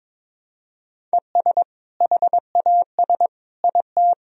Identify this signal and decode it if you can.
Morse code, heard as a keyed tone, about 22 wpm, 700 Hz. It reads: ES HAS IT
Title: Sentences from Top 100 Words and Abbreviations 22wpm